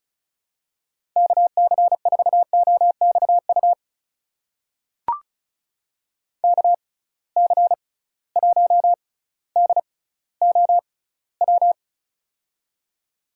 Morse code, heard as a keyed tone, about 36 wpm, spaced as for 18 wpm, 700 Hz. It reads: KC4OXU KC1DOW
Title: USA Call Signs - 6 Characters - 6x Character Spacing - 35wpm